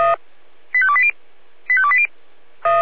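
Five-tone selective call (ZVEI), heard as quick runs of five tones, tones of about 70 ms. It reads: 85280 85280
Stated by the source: ZVEI five-tone selective call received by SDR.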